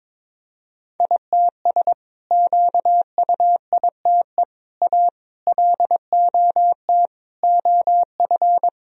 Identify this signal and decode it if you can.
Morse code, heard as a keyed tone, about 23 wpm, 700 Hz. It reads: ITS QUITE A LOT OF
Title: Sentences from Top 1000 Words - Review Part 3 - 22wpm